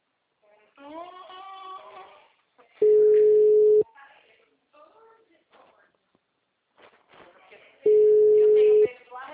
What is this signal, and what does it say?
Ringback tone (Europe, 425 Hz), with speech.